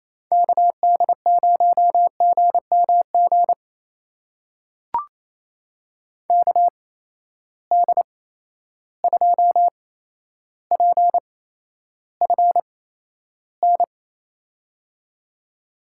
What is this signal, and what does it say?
Morse code, heard as a keyed tone, about 28 wpm, spaced as for 12 wpm, 700 Hz. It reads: KD0GMG KD2PFN
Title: USA Call Signs - 6 Characters - 8x Character Spacing - 28wpm